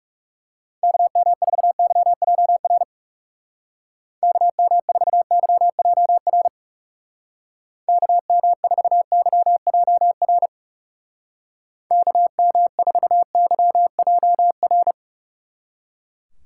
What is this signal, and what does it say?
Morse code, heard as a keyed tone, about 36 wpm, 700 Hz. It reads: KM4YJR KM4YJR KM4YJR KM4YJR
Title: Speed Racing - USA 2x3 Call Signs 30wpm